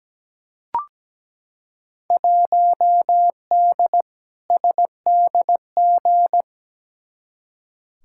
Morse code, heard as a keyed tone, about 17 wpm, 700 Hz. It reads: 1D SDG
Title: Field Day Exchanges - 17wpm